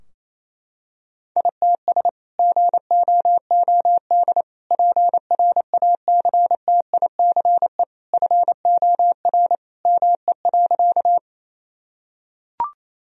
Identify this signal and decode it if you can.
Morse code, heard as a keyed tone, about 28 wpm, 700 Hz. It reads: ITS GOOD PRACTICE FOR ME.
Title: Sentences from Top 1000 Words - Review Part 4 - 28wpm